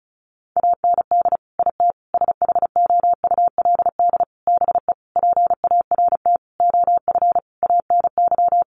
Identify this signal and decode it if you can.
Morse code, heard as a keyed tone, about 34 wpm, 700 Hz. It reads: AND IT SHOULD BE PART OF ANY
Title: Sentences from Top 1500 Words 35wpm